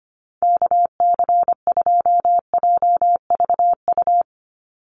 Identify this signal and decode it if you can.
Morse code, heard as a keyed tone, about 25 wpm, 700 Hz. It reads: KC2JVU
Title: USA Call Signs 25wpm